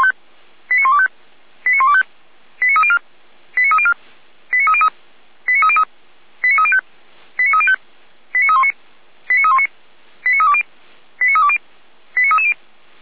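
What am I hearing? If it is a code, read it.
Five-tone selective call (ZVEI), heard as quick runs of five tones, tones of about 80 ms. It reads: 89125 89125 89304 89304 89303 89303 89305 89305 89319 89319 89320 89320 89330
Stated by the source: ZVEI five-tone call 89230 received by SDR, channel 455, recorded by feuerwehr-orlamuende fire brigade server.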